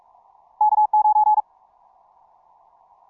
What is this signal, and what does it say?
Morse code, heard as a keyed tone, about 46 wpm, 850 Hz. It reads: K9
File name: simple-file-list / tx5k 15mcw2